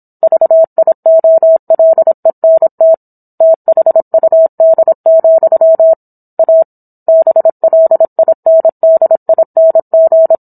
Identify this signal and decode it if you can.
Morse code, heard as a keyed tone, about 27 wpm, 650 Hz. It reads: VIOLENT THUD, A BLINDING